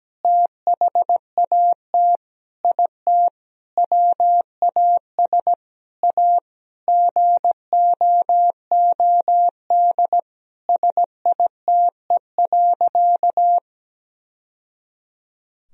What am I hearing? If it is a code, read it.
Morse code, heard as a keyed tone, about 17 wpm, 700 Hz. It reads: THAT IT WAS A GOOD SITE.
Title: Sentences from Top 1000 Words - Review Part 4 - 17wpm